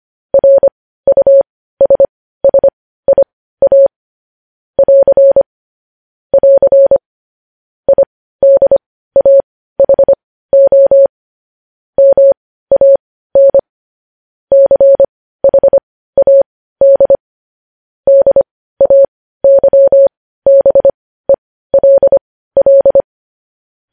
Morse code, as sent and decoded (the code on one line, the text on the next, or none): .-. ..- ... ... .. .- / .-.-. / .-.-. / .. -.. .- .... --- / -- .- -. / -.-. .... .- -.. / -.. .- -.-- -... .-.. .-..
RUSSIA <AR> <AR> IDAHO MAN CHAD DAYBELL